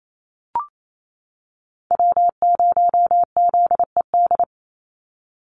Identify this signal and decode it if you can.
Morse code, heard as a keyed tone, about 28 wpm, 700 Hz. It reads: W0ZED
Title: USA Call Signs 28wpm